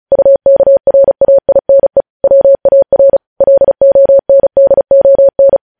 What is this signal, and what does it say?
Morse code, heard as a keyed tone, about 36 wpm, 550 Hz. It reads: UKRAINE WAR LONDON
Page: News Headlines in Morse Code at 35 WPM